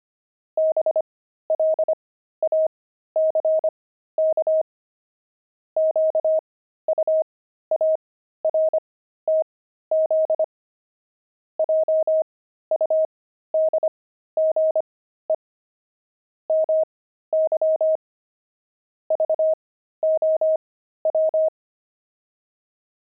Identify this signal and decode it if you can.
Morse code, heard as a keyed tone, about 25 wpm, spaced as for 13 wpm, 630 Hz. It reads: BLACK QUARTZ JUDGE MY VOW
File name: pangram-fs-13-cs-25-sphinx